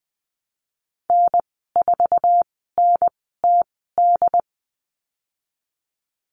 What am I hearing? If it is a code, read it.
Morse code, heard as a keyed tone, about 20 wpm, spaced as for 15 wpm, 700 Hz. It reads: N4NTD
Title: 069 Lesson 12g - Letter d - Call Signs 20wpm